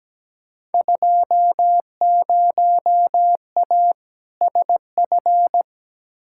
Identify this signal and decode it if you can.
Morse code, heard as a keyed tone, about 17 wpm, 700 Hz. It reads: 20A SF
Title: Field Day Exchanges Encore - 17wpm